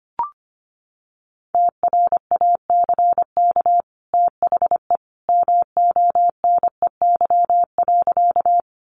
Morse code, heard as a keyed tone, about 25 wpm, 700 Hz. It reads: TRACK THE MONEY.
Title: Sentences from Top 1100 Words 25wpm